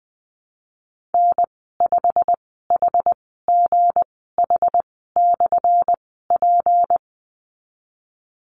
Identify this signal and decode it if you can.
Morse code, heard as a keyed tone, about 20 wpm, spaced as for 15 wpm, 700 Hz. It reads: N5HGH/P
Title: 179 Lesson 29g - Slash - Call Signs 20wpm